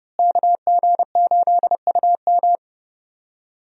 Morse code, heard as a keyed tone, about 32 wpm, 700 Hz. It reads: KG8UM
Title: USA Call Signs - 5 Characters - Repeated 2x - 30wpm